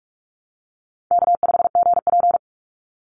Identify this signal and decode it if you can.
Morse code, heard as a keyed tone, about 44 wpm, 700 Hz. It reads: K5GP
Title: Call Signs Repeated 3x - 45wpm